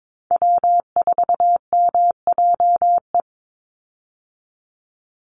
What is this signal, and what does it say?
Morse code, heard as a keyed tone, about 22 wpm, 700 Hz. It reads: W4MJE
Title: Call Signs Repeated 3x - 22wpm